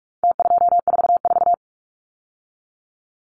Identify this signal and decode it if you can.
Morse code, heard as a keyed tone, about 44 wpm, 700 Hz. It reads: T244
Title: Numbers - Contest Style - 4-Digit 45wpm